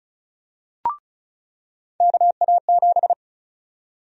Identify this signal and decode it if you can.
Morse code, heard as a keyed tone, about 34 wpm, 700 Hz. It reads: KA7
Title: USA Call Sign Prefixes - 35wpm